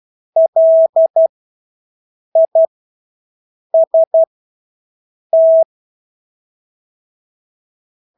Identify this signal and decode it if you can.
Morse code, heard as a keyed tone, about 12 wpm, spaced as for 6 wpm, 650 Hz. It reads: LIST
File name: Engelske ord 12-6 wpm